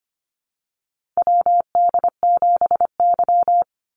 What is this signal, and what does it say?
Morse code, heard as a keyed tone, about 25 wpm, 700 Hz. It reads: WD7Y
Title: NA SOTA Chaser Callsigns 25wpm